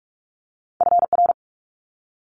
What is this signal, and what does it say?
Morse code, heard as a keyed tone, about 46 wpm, 700 Hz. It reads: FR
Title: Multi-letter Phonemes Quick Fire 45wpm